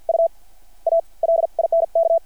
Morse code, heard as a keyed tone, about 50 wpm, spaced as for 43 wpm, 660 Hz. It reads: U A LINK
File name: archived / CRSnet